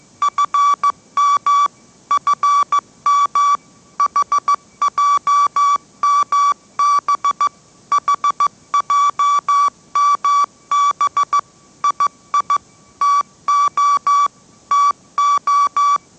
Morse code, heard as a keyed tone, about 15 wpm, 1150 Hz. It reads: FM FM HJMB HJMB II TO TO
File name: mrx / mrx065 16wpm